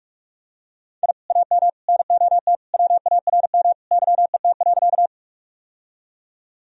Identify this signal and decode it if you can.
Morse code, heard as a keyed tone, about 41 wpm, 700 Hz. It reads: I AM NOT WARM YET.
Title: Sentences from Top 1000 Words - Review Part 2 - 45wpm